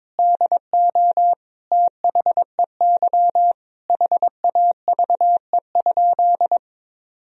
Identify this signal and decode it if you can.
Morse code, heard as a keyed tone, about 22 wpm, 700 Hz. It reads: DO THEY HAVE?
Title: Sentences from Top 1000 Words - Review Part 3 - 22wpm